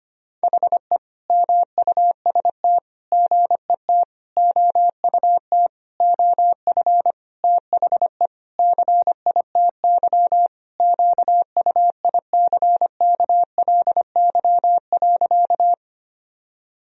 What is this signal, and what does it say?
Morse code, heard as a keyed tone, about 25 wpm, 700 Hz. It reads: HE MUST GET OUT OF THE CITY QUICKLY.
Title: Sentences from Top 1000 Words - Review Part 1 - 25wpm